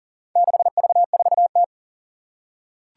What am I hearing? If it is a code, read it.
Morse code, heard as a keyed tone, about 42 wpm, 700 Hz. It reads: 6V4T
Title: Four Challenging Letter-Number Combinations 40wpm